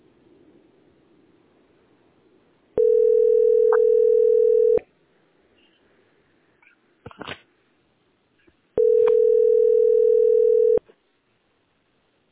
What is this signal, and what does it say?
Ringback tone (North America), 440 + 480 Hz.
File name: 8286bb70-7363-44b5-9615